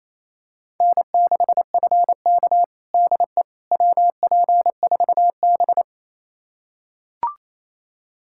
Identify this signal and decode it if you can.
Morse code, heard as a keyed tone, about 28 wpm, 700 Hz. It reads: N6FK DE WP4B
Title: USA Callsign DE Callsign - 4 Characters - 28wpm